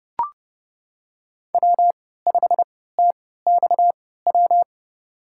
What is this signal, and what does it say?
Morse code, heard as a keyed tone, about 29 wpm, spaced as for 17 wpm, 700 Hz. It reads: W5TXW
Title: USA Call Signs - 3x Character Spacing - 30wpm